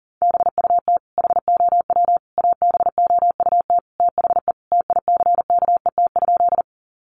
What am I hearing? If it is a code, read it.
Morse code, heard as a keyed tone, about 40 wpm, 700 Hz. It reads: BUT HOW ABOUT THE TICKET?
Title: Sentences from Top 1400 Words 40wpm